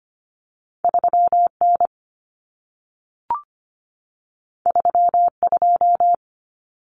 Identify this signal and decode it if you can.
Morse code, heard as a keyed tone, about 25 wpm, 700 Hz. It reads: 3N 32
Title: Common Cut Numbers - 2-Digit 25wpm